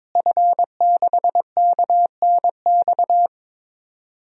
Morse code, heard as a keyed tone, about 22 wpm, 700 Hz. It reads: F6KNX
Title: DXCC-100 Call Signs - 5 Characters - Repeated 3x - 22wpm